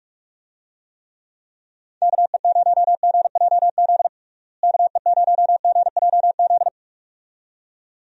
Morse code, heard as a keyed tone, about 46 wpm, 700 Hz. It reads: KE0GJZ KE0GJZ
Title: USA Call Signs - 6 Characters - Repeated 2x - 45wpm